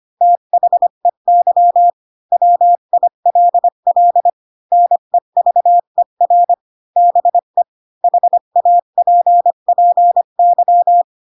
Morse code, heard as a keyed tone, about 25 wpm, 700 Hz. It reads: THEY WILL NEVER BE HAPPY
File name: Pinocchio / Pinocchio ch4 20 25 ik1hge